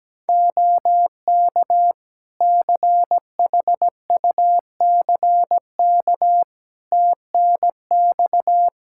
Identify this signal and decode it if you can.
Morse code, heard as a keyed tone, about 17 wpm, 700 Hz. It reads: OK CHUCK TNX
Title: Practice QSO - Exchange 4 - Ending - 17wpm